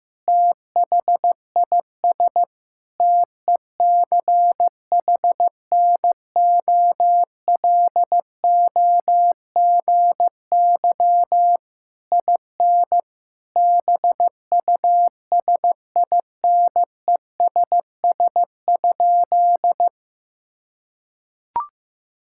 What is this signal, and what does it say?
Morse code, heard as a keyed tone, about 16 wpm, 700 Hz. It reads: THIS TECHNOLOGY IN BUSINESS?
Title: Sentences from Top 1000 Words - Review Part 5 - 15wpm